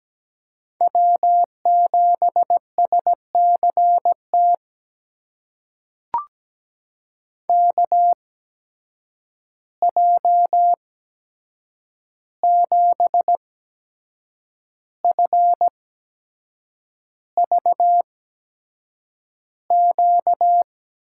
Morse code, heard as a keyed tone, about 17 wpm, spaced as for 7 wpm, 700 Hz. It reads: W7SCT KJ7FVQ
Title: USA Call Signs - 8x Character Spacing - 17wpm